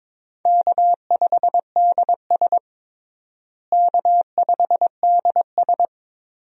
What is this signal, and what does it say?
Morse code, heard as a keyed tone, about 22 wpm, 700 Hz. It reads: K5DS K5DS
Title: USA Call Signs - 4 Characters - Repeated 3x - 22wpm